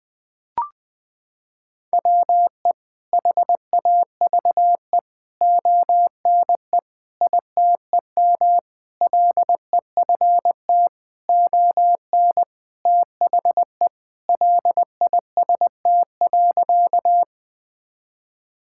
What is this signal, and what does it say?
Morse code, heard as a keyed tone, about 20 wpm, 700 Hz. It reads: WE HAVE ONE ITEM LEFT ON THE LIST.
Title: Sentences from Top 1000 Words - Review Part 6 - 20wpm